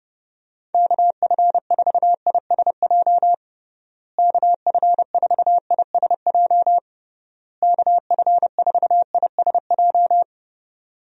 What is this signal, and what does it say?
Morse code, heard as a keyed tone, about 32 wpm, 700 Hz. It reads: KF4ISJ KF4ISJ KF4ISJ
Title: USA Call Signs - 6 Characters - Repeated 3x - 30wpm